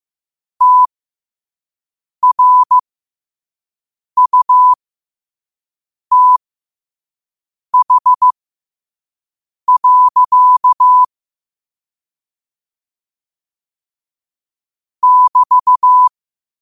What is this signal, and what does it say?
Morse code, heard as a keyed tone, about 15 wpm, spaced as for 5 wpm, 1000 Hz. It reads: TRUTH. <BT>
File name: 5wpm